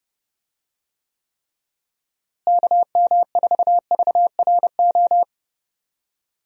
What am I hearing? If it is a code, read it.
Morse code, heard as a keyed tone, about 30 wpm, 700 Hz. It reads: KM4VRO
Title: USA Call Signs - 6 Characters - Repeated 3x - 30wpm